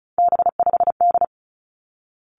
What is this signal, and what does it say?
Morse code, heard as a keyed tone, about 36 wpm, 700 Hz. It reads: B5D
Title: Three Challenging Letter-Number Combinations 35wpm